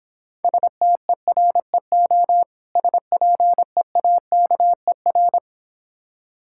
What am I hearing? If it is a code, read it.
Morse code, heard as a keyed tone, about 26 wpm, 690 Hz. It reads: STEREO SPEAKER